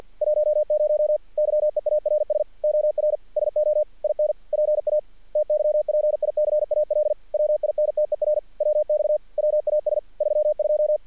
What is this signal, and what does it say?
Morse code, heard as a keyed tone, about 46 wpm, 610 Hz. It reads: 10 YEARS OR SO IN PA TYPICAL WINTER WX WAS 32
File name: archived / CRSnet